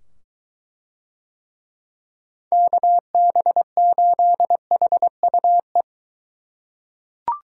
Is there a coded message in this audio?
Yes: Morse code from 2.5 to 5.8 s, heard as a keyed tone, about 23 wpm, 700 Hz. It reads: KB8HUE